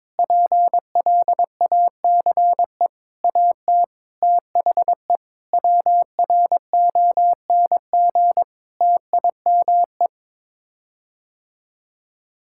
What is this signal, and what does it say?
Morse code, heard as a keyed tone, about 22 wpm, 700 Hz. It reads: PLACE AT THE WRONG TIME
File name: practice / 22wpm / Sets of 8 Words - 22wpm